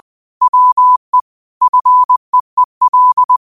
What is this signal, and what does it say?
Morse code, heard as a keyed tone, about 20 wpm, 1000 Hz. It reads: WE FEEL
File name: qotdfiles / 20wpm